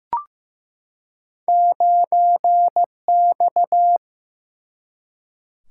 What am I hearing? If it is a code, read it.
Morse code, heard as a keyed tone, about 15 wpm, 700 Hz. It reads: 9X
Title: Single-Letter Single-Number 15wpm